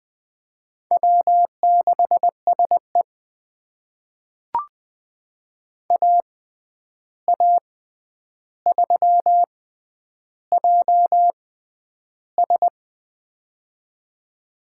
Morse code, heard as a keyed tone, about 20 wpm, spaced as for 10 wpm, 700 Hz. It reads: W6SE AA3JS
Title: USA Call Signs - 6x Character Spacing - 20wpm